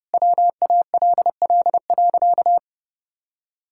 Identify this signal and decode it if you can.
Morse code, heard as a keyed tone, about 30 wpm, 700 Hz. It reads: WALL.
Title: Sentences from Top 1600 Words 30wpm